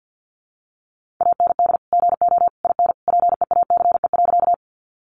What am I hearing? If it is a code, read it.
Morse code, heard as a keyed tone, about 50 wpm, 700 Hz. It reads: AND GO IN PEACE.
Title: Sentences from Top 1500 Words 50wpm